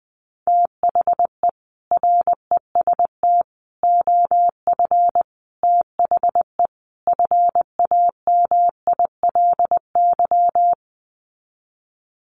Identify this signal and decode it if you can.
Morse code, heard as a keyed tone, about 20 wpm, 700 Hz. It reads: THE REST OF THE FAMILY
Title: Sets of 5 Words Encore2 20wpm